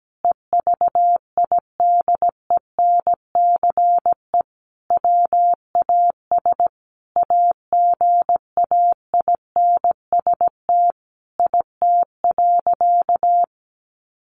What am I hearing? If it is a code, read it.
Morse code, heard as a keyed tone, about 17 wpm, 700 Hz. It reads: EVIDENCE WAS AGAINST IT.
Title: Sentences from Top 1400 Words 17wpm